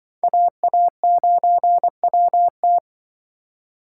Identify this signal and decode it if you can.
Morse code, heard as a keyed tone, about 25 wpm, 700 Hz. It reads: AA9WT
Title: Speed Racing - USA 2x2 Call Signs 20wpm